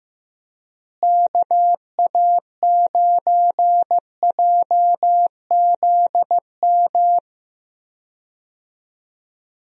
Morse code, heard as a keyed tone, about 16 wpm, 700 Hz. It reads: KA9JZM
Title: USA Call Signs - 6 Characters - Repeated 2x - 15wpm